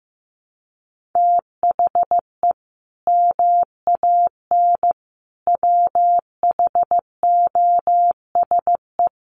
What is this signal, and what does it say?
Morse code, heard as a keyed tone, about 15 wpm, 700 Hz. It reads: THE MAN WHOSE
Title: Sentences from Top 1200 Words 15wpm